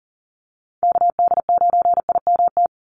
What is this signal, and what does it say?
Morse code, heard as a keyed tone, about 40 wpm, 700 Hz. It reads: KD9IMT
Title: USA Call Signs 40wpm